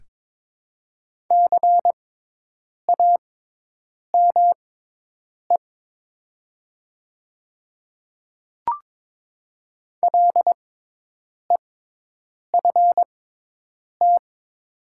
Morse code, heard as a keyed tone, about 22 wpm, spaced as for 8 wpm, 700 Hz. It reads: CAME LEFT
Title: 6x Character Spacing - Top 500 Words - 22wpm